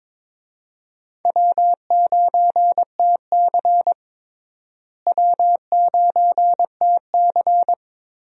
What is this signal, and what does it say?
Morse code, heard as a keyed tone, about 22 wpm, 700 Hz. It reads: W9TC W9TC
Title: USA Call Signs - 4 Characters - Repeated 2x - 22wpm